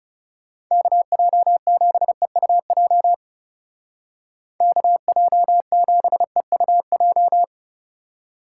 Morse code, heard as a keyed tone, about 36 wpm, 700 Hz. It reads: KJ7EUJ KJ7EUJ
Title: Speed Racing - USA 2x3 Call Signs 30wpm